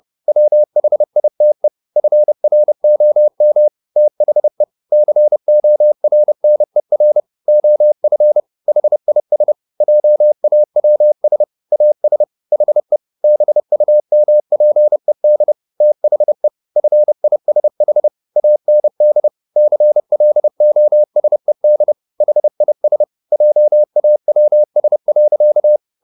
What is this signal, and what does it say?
Morse code, heard as a keyed tone, about 30 wpm, 600 Hz. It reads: WHITE FROM THE CORNER OF HIS JAWS AS HE BUMPED THE FISH AND CLOSED HIS JAWS.